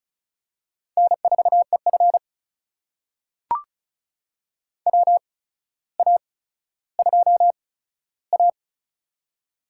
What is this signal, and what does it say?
Morse code, heard as a keyed tone, about 36 wpm, spaced as for 15 wpm, 700 Hz. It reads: N4EF WA2A
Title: USA Call Signs - 4 Characters - 8x Character Spacing - 35wpm